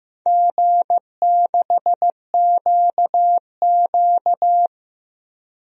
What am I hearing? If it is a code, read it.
Morse code, heard as a keyed tone, about 16 wpm, 700 Hz. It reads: G6QQ
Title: DXCC-100 Call Signs - 4 Characters - 15wpm